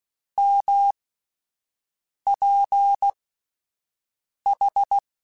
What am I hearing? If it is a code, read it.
Morse code, heard as a keyed tone, about 16 wpm, spaced as for 5 wpm, 800 Hz. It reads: MPH
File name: Spaced Characters / Spaced Characters-mp3s / Weather 4 10 sp @10